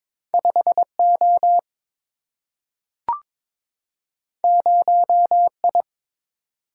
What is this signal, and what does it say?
Morse code, heard as a keyed tone, about 22 wpm, 700 Hz. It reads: 5O 0I
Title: Single-Letter Single-Number 22wpm